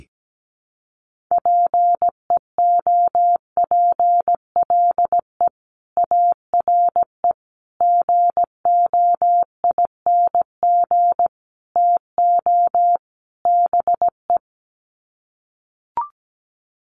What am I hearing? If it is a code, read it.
Morse code, heard as a keyed tone, about 17 wpm, 700 Hz. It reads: PEOPLE ARE GOING TO BE
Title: Sets of 5 Words Encore 17wpm